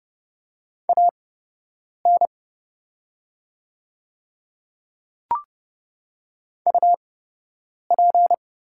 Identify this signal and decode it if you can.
Morse code, heard as a keyed tone, about 30 wpm, spaced as for 8 wpm, 700 Hz. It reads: AN UP
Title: 8x Character Spacing - Top 100 Words - 30wpm